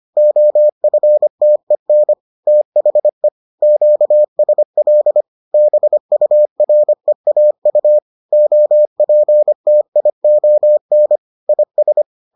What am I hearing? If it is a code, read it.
Morse code, heard as a keyed tone, about 25 wpm, 600 Hz. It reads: OFTEN THE QSL BUREAU OPTION IS